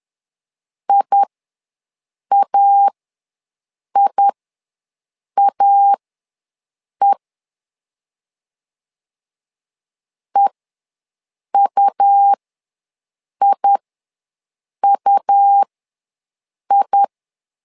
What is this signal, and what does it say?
Morse code, heard as a keyed tone, about 11 wpm, spaced as for 6 wpm, 800 Hz. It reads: IAIAE EUIUI
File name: LEZIONI / 2 UHDN / 2 U ws 10wpm 2017